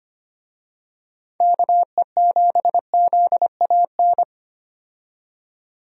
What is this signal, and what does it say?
Morse code, heard as a keyed tone, about 25 wpm, 700 Hz. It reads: KE7ZAN
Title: USA Call Signs - 6 Characters - 25wpm